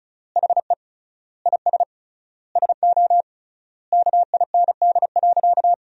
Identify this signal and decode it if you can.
Morse code, heard as a keyed tone, about 36 wpm, 700 Hz. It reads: HE IS SO KIND.
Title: 3x Word-Spacing - Sentences from Top 500 Words - 35wpm